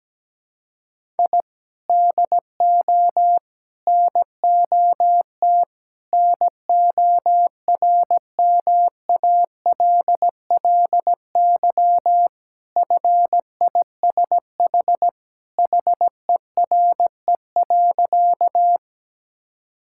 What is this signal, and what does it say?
Morse code, heard as a keyed tone, about 17 wpm, 700 Hz. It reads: I DO NOT NORMALLY FISH HERE.
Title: Sentences from Top 1000 Words - Review Part 6 - 17wpm